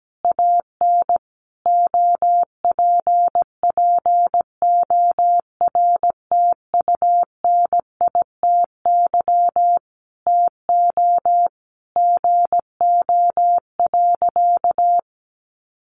Morse code, heard as a keyed tone, about 17 wpm, 700 Hz. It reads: AN OPPORTUNITY TO GO.
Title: Sentences from Top 1000 Words 17wpm